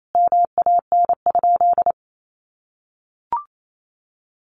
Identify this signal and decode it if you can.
Morse code, heard as a keyed tone, about 28 wpm, 700 Hz. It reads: MAN?
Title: Sentences from Top 1400 Words 28wpm